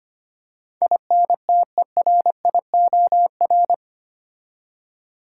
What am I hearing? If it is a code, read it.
Morse code, heard as a keyed tone, about 25 wpm, 700 Hz. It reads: INTERIOR
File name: 8 Letter Words - First 3 Letters Given - 25wpm